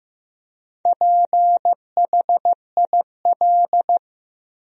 Morse code, heard as a keyed tone, about 15 wpm, 700 Hz. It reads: PHIL
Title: USA Common Names - Max 4 Letters - 15wpm